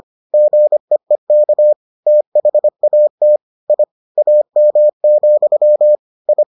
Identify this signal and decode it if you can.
Morse code, heard as a keyed tone, about 25 wpm, 600 Hz. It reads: GEEK THAT I AM, I